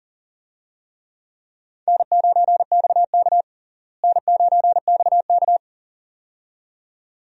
Morse code, heard as a keyed tone, about 40 wpm, 700 Hz. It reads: N9XK N9XK
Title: USA Call Signs - 4 Characters - Repeated 2x - 40wpm